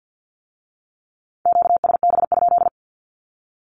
Morse code, heard as a keyed tone, about 50 wpm, 700 Hz. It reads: QSB?
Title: Q-Codes with Spoken Meaning - 50wpm